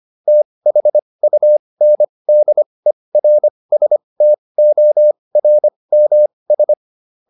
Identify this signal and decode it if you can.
Morse code, heard as a keyed tone, about 25 wpm, spaced as for 20 wpm, 600 Hz. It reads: THUNDERSTORMS